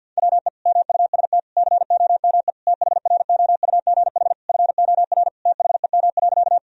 Morse code, heard as a keyed tone, about 46 wpm, 700 Hz. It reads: WE MUST COME THROUGH FOR THEM.
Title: Sentences from Top 1000 Words - Review Part 1 - 50wpm